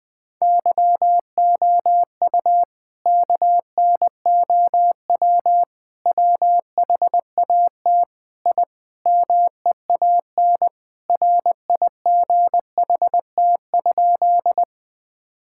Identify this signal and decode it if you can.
Morse code, heard as a keyed tone, about 20 wpm, 700 Hz. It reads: YOU KNOW WHAT I MEAN RIGHT?
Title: Sentences with 5-Word Phrases - 20wpm